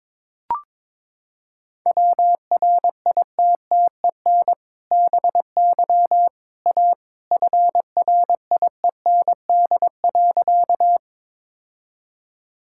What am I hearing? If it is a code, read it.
Morse code, heard as a keyed tone, about 22 wpm, 700 Hz. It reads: WRITTEN BY A FRIEND.
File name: practice / 22wpm / Sentences with 2-Word Phrases - 22wpm